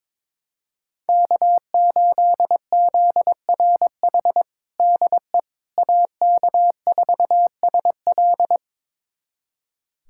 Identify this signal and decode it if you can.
Morse code, heard as a keyed tone, about 21 wpm, 700 Hz. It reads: K8ZRH DE AK4SL
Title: USA Callsign DE Callsign - 5 Characters - 22wpm